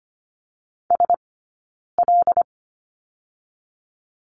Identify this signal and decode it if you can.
Morse code, heard as a keyed tone, about 25 wpm, 700 Hz. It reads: S L
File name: course / 25:10wpm / 092 Lesson 16d - Letter c - Two Characters 25wpm